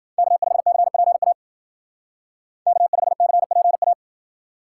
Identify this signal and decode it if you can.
Morse code, heard as a keyed tone, about 52 wpm, 700 Hz. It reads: K5CPA K5CPA